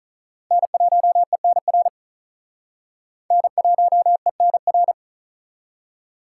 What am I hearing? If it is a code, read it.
Morse code, heard as a keyed tone, about 45 wpm, 700 Hz. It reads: N1ENR N1ENR